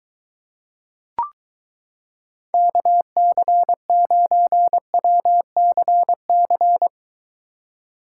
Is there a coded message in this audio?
Yes: Morse code from 2.5 to 6.9 s, heard as a keyed tone, about 23 wpm, 700 Hz. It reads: KC9WCC